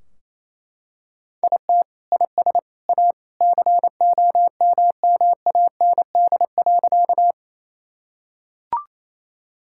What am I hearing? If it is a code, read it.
Morse code, heard as a keyed tone, about 28 wpm, 700 Hz. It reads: IT IS A COMMAND.
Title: Sentences from Top 1600 Words 28wpm